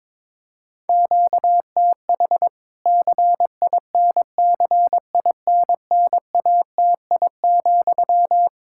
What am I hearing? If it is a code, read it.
Morse code, heard as a keyed tone, about 21 wpm, 700 Hz. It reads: QTH CINCINNATI,
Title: K1USN SST QSO Practice 22wpm